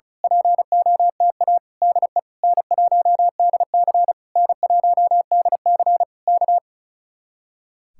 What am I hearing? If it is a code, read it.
Morse code, heard as a keyed tone, about 32 wpm, 700 Hz. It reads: POTA DE N1DC N1DC K
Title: Speed-Racing Common QSO Phrases Encore - 35wpm